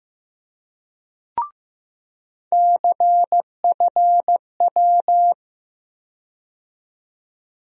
Morse code, heard as a keyed tone, about 15 wpm, 700 Hz. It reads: CFW
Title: USA Call Sign Postfixes - 3 Letters 15wpm